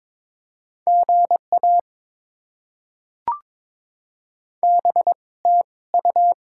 Morse code, heard as a keyed tone, about 22 wpm, spaced as for 18 wpm, 690 Hz. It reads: GA BTU
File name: Top 20 QSO Elements - 2x Character Spacing - 22wpm